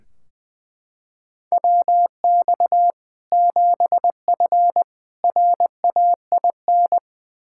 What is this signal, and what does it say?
Morse code, heard as a keyed tone, about 20 wpm, 700 Hz. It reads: WX 7F RAIN
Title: K1USN SST QSO Practice 20wpm